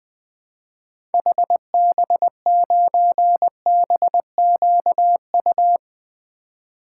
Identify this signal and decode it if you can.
Morse code, heard as a keyed tone, about 20 wpm, 700 Hz. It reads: HB9BQU
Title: 2023 SOTA Activator Call Signs - 20wpm